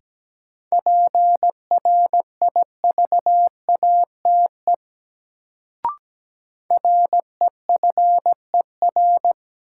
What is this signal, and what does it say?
Morse code, heard as a keyed tone, about 17 wpm, 700 Hz. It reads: PRIVATE REFER
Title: Sentences from Top 1000 Words - New Word Drill - 17wpm